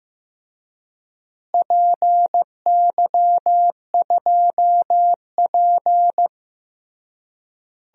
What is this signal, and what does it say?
Morse code, heard as a keyed tone, about 15 wpm, 700 Hz. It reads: PY2P
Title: DXCC-100 Call Signs - 4 Characters - Repeated 3x - 15wpm